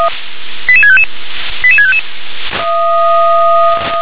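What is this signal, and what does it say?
Five-tone selective call (ZVEI), heard as quick runs of five tones, tones of about 80 ms. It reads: 88655 88655
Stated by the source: ZVEI five-tone selective call 88768 received on an SDR.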